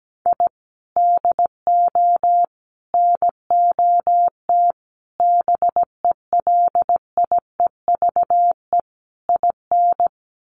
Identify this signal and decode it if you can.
Morse code, heard as a keyed tone, about 17 wpm, 700 Hz. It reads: I DO NOT BELIEVE IN
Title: Sets of 5 Words Encore 17wpm